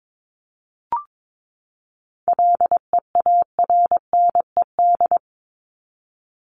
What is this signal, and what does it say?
Morse code, heard as a keyed tone, about 22 wpm, 700 Hz. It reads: LEARNED
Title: ED Suffix - Top 100 Words with Base Words 22wpm